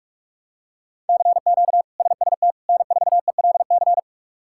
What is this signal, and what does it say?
Morse code, heard as a keyed tone, about 41 wpm, 700 Hz. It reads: CQ SST N4ELC
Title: K1USN SST QSO Practice - 45wpm